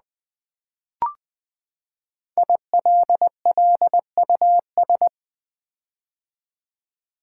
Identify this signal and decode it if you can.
Morse code, heard as a keyed tone, about 20 wpm, 700 Hz. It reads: ILLUS
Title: Build A Word - 5 Letters At A Time - 10-Letter Words - 20wpm